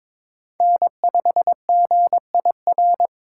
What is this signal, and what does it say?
Morse code, heard as a keyed tone, about 26 wpm, 700 Hz. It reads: N5GIR
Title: Speed Racing - USA 1x3 Call Signs 22wpm